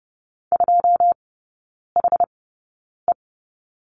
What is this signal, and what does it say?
Morse code, heard as a keyed tone, about 30 wpm, spaced as for 9 wpm, 700 Hz. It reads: 2HE
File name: course / 30:10wpm / 087 Lesson 15e - Number 5 - Three Characters 30wpm